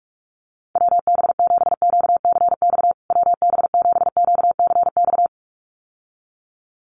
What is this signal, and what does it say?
Morse code, heard as a keyed tone, about 44 wpm, 700 Hz. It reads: WB7QCX WB7QCX
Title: USA Call Signs Repeated 2x 45wpm